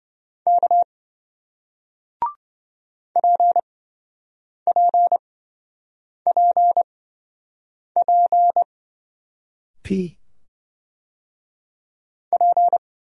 Morse code, heard as a keyed tone, about 26 wpm, spaced as for 7 wpm, 700 Hz. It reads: K PPPP P